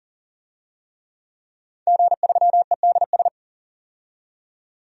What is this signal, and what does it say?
Morse code, heard as a keyed tone, about 39 wpm, 700 Hz. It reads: G3EDS
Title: DXCC-100 Call Signs - 5 Characters - 40wpm